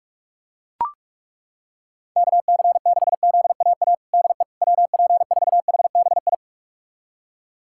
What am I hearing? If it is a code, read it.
Morse code, heard as a keyed tone, about 41 wpm, 690 Hz. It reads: KC6ZAA DE WP4HBI
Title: USA Callsign DE Callsign - 6 Characters - 45wpm